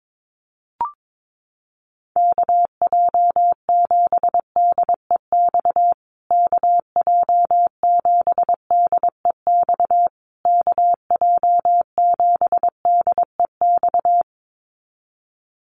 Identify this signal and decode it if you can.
Morse code, heard as a keyed tone, about 22 wpm, 700 Hz. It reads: KJ7DEX KJ7DEX KJ7DEX
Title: USA Call Signs Repeated 3x 22wpm